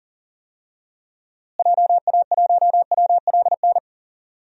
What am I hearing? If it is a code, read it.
Morse code, heard as a keyed tone, about 40 wpm, 700 Hz. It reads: JA1WLN
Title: DXCC-100 Call Signs - 6 Characters - Repeated 2x - 40wpm